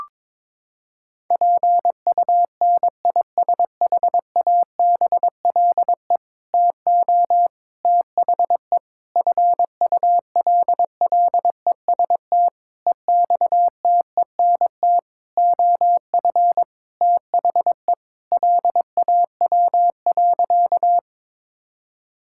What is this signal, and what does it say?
Morse code, heard as a keyed tone, about 22 wpm, 700 Hz. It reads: PUNISHABLE TO THE FULLEST EXTENT OF THE LAW.
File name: Sentences with 7-Word Phrases - 22wpm